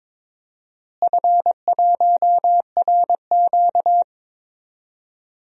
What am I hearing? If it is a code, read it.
Morse code, heard as a keyed tone, about 21 wpm, 700 Hz. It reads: F1RQ
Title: DXCC-100 Call Signs - 4 Characters - Repeated 2x - 22wpm